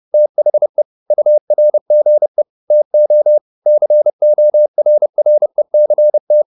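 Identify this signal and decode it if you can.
Morse code, heard as a keyed tone, about 30 wpm, 600 Hz. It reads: THE URGE TO CORRECT
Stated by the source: Morse code at 30 wpm, the novel Never by Ken Follett in English, from CW med Gnister.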